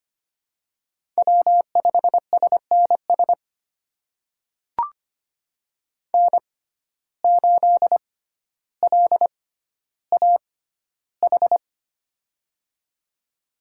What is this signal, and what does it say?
Morse code, heard as a keyed tone, about 25 wpm, spaced as for 13 wpm, 700 Hz. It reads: W5SNS N8LAH
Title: USA Call Signs - 5 Characters - 6x Character Spacing - 25wpm